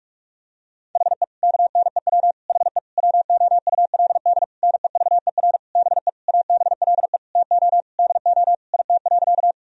Morse code, heard as a keyed tone, about 42 wpm, 700 Hz. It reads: HE KNEW HE WOULD NEVER BE ABLE TO DO IT.
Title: Sentences from Top 1000 Words - Review Part 2 - 45wpm